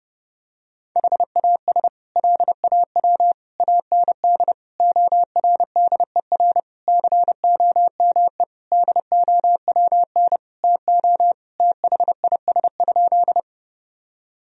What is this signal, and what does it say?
Morse code, heard as a keyed tone, about 30 wpm, 700 Hz. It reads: HAS LAW AND ORDER COME DOWN TO THIS?
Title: Sentences from Top 1000 Words - Review Part 3 - 30wpm